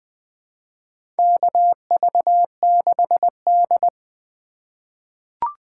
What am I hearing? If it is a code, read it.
Morse code, heard as a keyed tone, about 21 wpm, 700 Hz. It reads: KV6D
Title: USA Call Signs - 3x Character Spacing - 20wpm